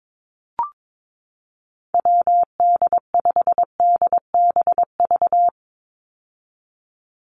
Morse code, heard as a keyed tone, about 22 wpm, 700 Hz. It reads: WD5DBV
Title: Call Signs - 22wpm